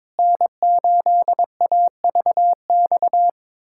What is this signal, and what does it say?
Morse code, heard as a keyed tone, about 22 wpm, 700 Hz. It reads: N8AVX
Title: USA Callsign DE Callsign 22wpm